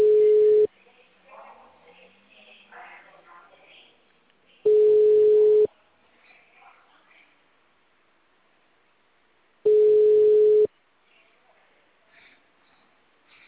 Ringback tone (Europe, 425 Hz).